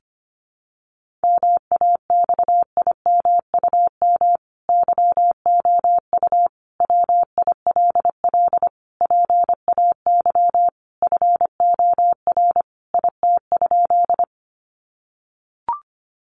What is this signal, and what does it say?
Morse code, heard as a keyed tone, about 25 wpm, 700 Hz. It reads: MAXIMUM YOU WILL PAY FOR IT?
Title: Sentences from Top 1300 Words 25wpm